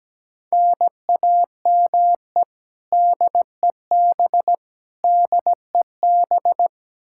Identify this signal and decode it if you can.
Morse code, heard as a keyed tone, about 17 wpm, 700 Hz. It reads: NAME DEB DEB
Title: Practice QSO - Exchange 1 - Beginning - 17wpm